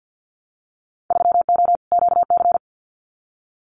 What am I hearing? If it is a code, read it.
Morse code, heard as a keyed tone, about 50 wpm, 700 Hz. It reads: 3O QC
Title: Winter Field Day Exchanges Encore 50wpm